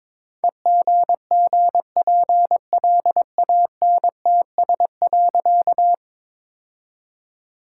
Morse code, heard as a keyed tone, about 22 wpm, 700 Hz. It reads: EGGPLANTS.
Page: morse-code-ninja-practice-22wpm